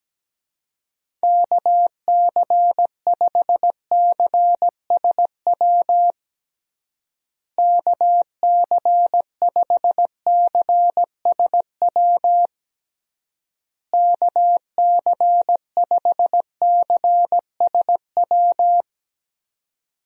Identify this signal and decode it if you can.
Morse code, heard as a keyed tone, about 17 wpm, 700 Hz. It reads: KC5CSW KC5CSW KC5CSW
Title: USA Call Signs - 6 Characters - Repeated 3x - 17wpm